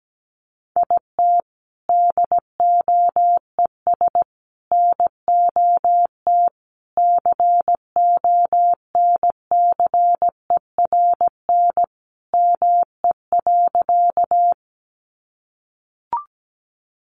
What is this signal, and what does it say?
Morse code, heard as a keyed tone, about 17 wpm, 700 Hz. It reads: IT DOES NOT CONCERN ME.
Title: Sentences from Top 1000 Words 17wpm